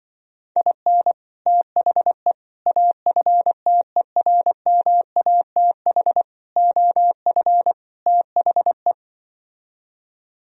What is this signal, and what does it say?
Morse code, heard as a keyed tone, about 24 wpm, 700 Hz. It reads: IN THE AFTERMATH OF THE
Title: Speed-Racing - Sets of 5 Words 20wpm